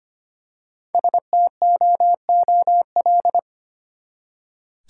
Morse code, heard as a keyed tone, about 25 wpm, 700 Hz. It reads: STOOL
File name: practice / 25wpm / Easy Words with Hints - 25wpm